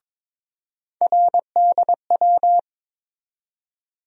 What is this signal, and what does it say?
Morse code, heard as a keyed tone, about 22 wpm, 700 Hz. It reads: RDW
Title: USA Call Sign Postfixes - 3 Letters 22wpm